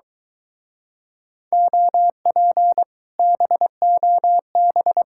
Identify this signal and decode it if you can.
Morse code, heard as a keyed tone, about 23 wpm, 700 Hz. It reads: OP BOB